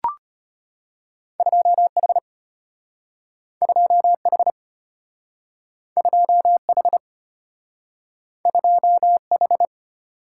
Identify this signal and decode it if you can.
Morse code, heard as a keyed tone, about 34 wpm, 700 Hz. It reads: 2H 2H 2H 2H